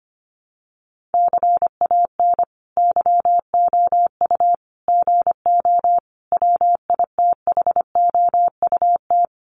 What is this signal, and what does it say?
Morse code, heard as a keyed tone, about 25 wpm, 700 Hz. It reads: CAN YOU GO WITHOUT
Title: Sentences from Top 1500 Words 25wpm